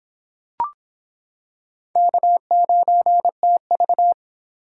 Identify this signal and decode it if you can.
Morse code, heard as a keyed tone, about 24 wpm, 700 Hz. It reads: K9TV